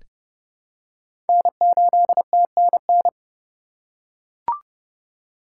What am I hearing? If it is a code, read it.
Morse code, heard as a keyed tone, about 30 wpm, 700 Hz. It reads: N8TNN
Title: USA Call Signs - 2x Character Spacing - 30wpm